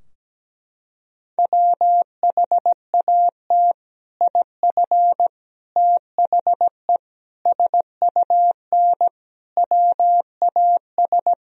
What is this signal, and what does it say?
Morse code, heard as a keyed tone, about 17 wpm, 700 Hz. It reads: WHAT IF THE SUN WAS
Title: Sentences from Top 1000 Words - Review Part 2 - 17wpm